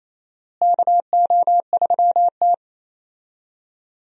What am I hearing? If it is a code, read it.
Morse code, heard as a keyed tone, about 28 wpm, 700 Hz. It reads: KO3T
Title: USA Call Signs - 4 Characters - Repeated 2x - 28wpm